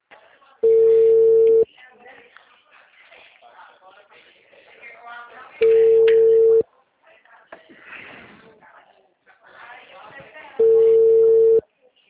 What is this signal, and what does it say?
Ringback tone (China, 450 Hz), with speech.